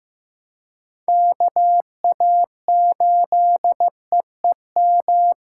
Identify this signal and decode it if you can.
Morse code, heard as a keyed tone, about 18 wpm, 700 Hz. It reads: KA8EEM
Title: Speed Racing - USA 2x3 Call Signs 15wpm